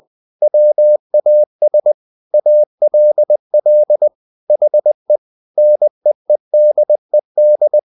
Morse code, heard as a keyed tone, about 20 wpm, 600 Hz. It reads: WAS ALL HE NEEDED
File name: the old man and the sea / the old man and the sea part 1 20wpm